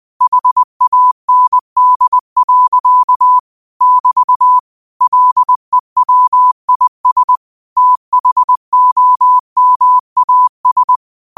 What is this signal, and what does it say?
Morse code, heard as a keyed tone, about 20 wpm, 1000 Hz. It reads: HAND. <BT> LEWIS THOMAS
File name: qotdfiles / 20wpm